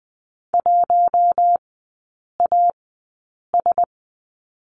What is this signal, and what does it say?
Morse code, heard as a keyed tone, about 20 wpm, spaced as for 8 wpm, 700 Hz. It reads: 1AS
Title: 100 Lesson 17e - Letter u - Three Characters 20wpm